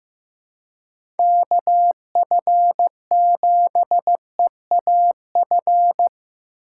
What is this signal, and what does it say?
Morse code, heard as a keyed tone, about 16 wpm, 700 Hz. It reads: KF7EAF
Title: USA Call Signs - 6 Characters - 2x Character Spacing - 15wpm